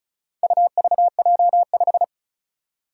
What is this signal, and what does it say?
Morse code, heard as a keyed tone, about 36 wpm, 700 Hz. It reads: UVJ5
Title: Four Challenging Letter-Number Combinations 35wpm